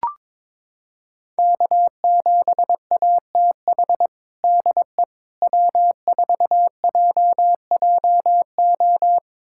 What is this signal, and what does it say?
Morse code, heard as a keyed tone, about 22 wpm, 700 Hz. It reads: K7ATH DE W4JJO
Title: USA Callsign DE Callsign - 5 Characters - 22wpm